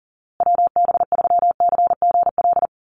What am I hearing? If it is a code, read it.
Morse code, heard as a keyed tone, about 40 wpm, 700 Hz. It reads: WB3CGL
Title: USA Call Signs Repeated 2x 40wpm